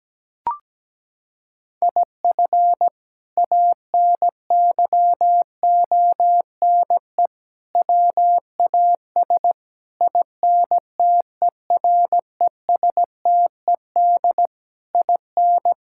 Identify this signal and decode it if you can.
Morse code, heard as a keyed tone, about 17 wpm, 700 Hz. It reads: IF ANYONE WAS INTERESTED IN
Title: Sentences from Top 1600 Words 17wpm